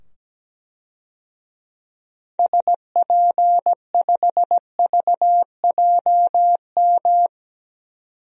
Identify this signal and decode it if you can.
Morse code, heard as a keyed tone, about 17 wpm, 700 Hz. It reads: SP5VJM
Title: DXCC-100 Call Signs - 6 Characters - 17wpm